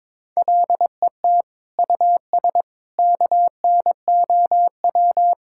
Morse code, heard as a keyed tone, about 22 wpm, 700 Hz. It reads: LET US KNOW
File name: practice / 22wpm / Sets of 7 Words - 22wpm